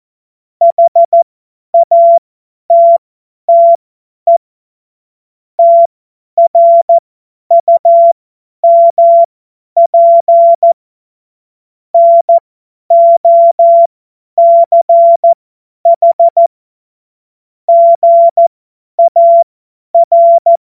Morse code, heard as a keyed tone, about 14 wpm, spaced as for 10 wpm, 680 Hz. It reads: HATTE TRUMP NOCH GAR